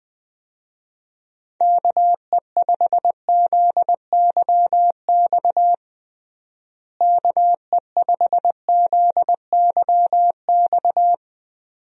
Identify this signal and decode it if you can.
Morse code, heard as a keyed tone, about 20 wpm, 700 Hz. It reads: KE5ZYX KE5ZYX
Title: USA Call Signs - 6 Characters - Repeated 2x - 20wpm